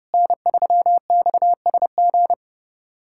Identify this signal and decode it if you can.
Morse code, heard as a keyed tone, about 30 wpm, 700 Hz. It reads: N3XSG
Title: USA Callsign DE Callsign - 5 Characters - 30wpm